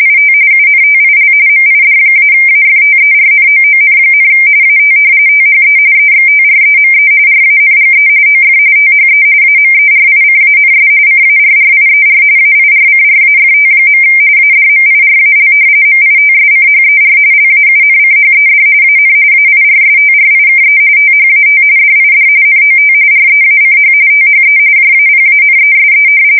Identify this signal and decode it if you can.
RTTY, about 45.45 baud. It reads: IT UNTIL MID-AFTERNOON, AT THE EXTREME RANGE OF THE AMERICANS CARRIER AIRCRAFT. ADMIRAL MITSCHER LAUNCHED AN AIRSTRIKE THAT DID NOT REACH THE JAPANESE
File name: fsk170 / battleshipnj